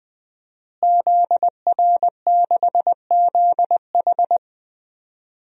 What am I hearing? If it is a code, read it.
Morse code, heard as a keyed tone, about 21 wpm, 700 Hz. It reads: ZR6ZH
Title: DXCC-100 Call Signs - 5 Characters - Repeated 3x - 20wpm